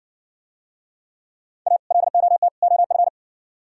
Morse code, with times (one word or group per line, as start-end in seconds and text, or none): 1.7-1.8 A
1.9-2.5 LOT
2.6-3.1 OF